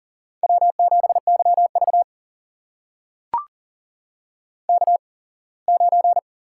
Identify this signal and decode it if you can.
Morse code, heard as a keyed tone, about 40 wpm, spaced as for 24 wpm, 700 Hz. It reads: W7YV K9
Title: USA Call Signs - 8x Character Spacing - 40wpm